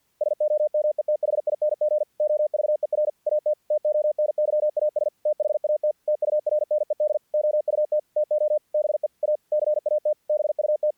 Morse code, heard as a keyed tone, about 48 wpm, 600 Hz. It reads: SOMETHING OVER AT TONYAS THAT TURNED OUT TO BE A CAT BUT
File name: archived / CRSnet